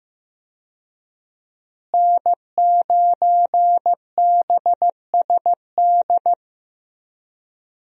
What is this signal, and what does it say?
Morse code, heard as a keyed tone, about 15 wpm, 700 Hz. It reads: N9BSD
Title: USA Call Signs - 5 Characters - Repeated 2x - 15wpm